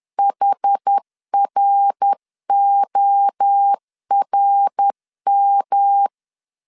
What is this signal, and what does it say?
Morse code, heard as a keyed tone, about 10 wpm, 800 Hz. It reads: HRORM
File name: CW / DET / 1 3 1500C S A 10wpm 3447